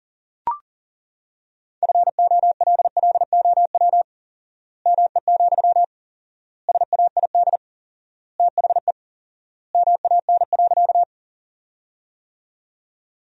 Morse code, heard as a keyed tone, about 37 wpm, 700 Hz. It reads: FOLLOW ME, SAID THE MAN.
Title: 4x Word-Spacing - Sentences from Top 200 Words - 40wpm